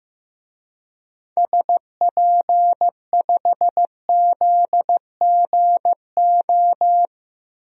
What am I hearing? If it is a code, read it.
Morse code, heard as a keyed tone, about 15 wpm, 700 Hz. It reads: SP5ZGO
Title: DXCC-100 Call Signs - 6 Characters - 15wpm